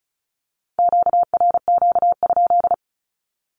Morse code, heard as a keyed tone, about 36 wpm, 700 Hz. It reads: QRQ?
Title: Q-Codes with Spoken Meaning - 35wpm